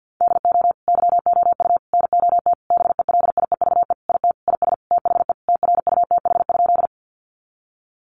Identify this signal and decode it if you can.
Morse code, heard as a keyed tone, about 50 wpm, 700 Hz. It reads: DO YOU NOT BELIEVE IT IS THE TRUTH?
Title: Sentences from Top 1000 Words 50wpm